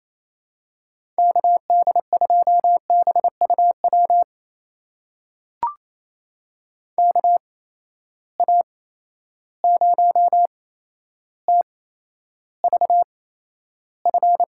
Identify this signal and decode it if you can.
Morse code, heard as a keyed tone, about 28 wpm, spaced as for 12 wpm, 700 Hz. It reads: KD2BUW KA0TVF
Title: USA Call Signs - 6 Characters - 8x Character Spacing - 28wpm